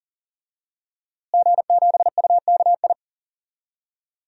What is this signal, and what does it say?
Morse code, heard as a keyed tone, about 40 wpm, 700 Hz. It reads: G7UKI
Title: DXCC-100 Call Signs - 5 Characters - Repeated 3x - 40wpm